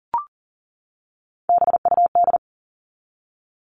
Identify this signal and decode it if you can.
Morse code, heard as a keyed tone, about 40 wpm, 700 Hz. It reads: BUD
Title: All Cut Numbers - 3-Digit 40wpm